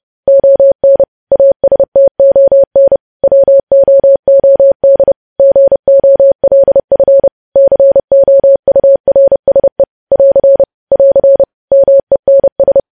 Morse code, heard as a keyed tone, about 30 wpm, 550 Hz. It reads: ON ASTON WOOD GOLF COURSE <AR> <AR> MENS
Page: News Headlines in Morse Code at 30 WPM